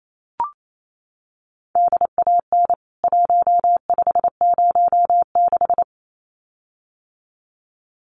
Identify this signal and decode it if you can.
Morse code, heard as a keyed tone, about 28 wpm, 700 Hz. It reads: DAN 1506
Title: CWT Exchange 28wpm